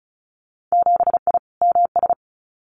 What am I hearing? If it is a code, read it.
Morse code, heard as a keyed tone, about 34 wpm, 700 Hz. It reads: 7I MS
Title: Winter Field Day Exchanges Encore 35wpm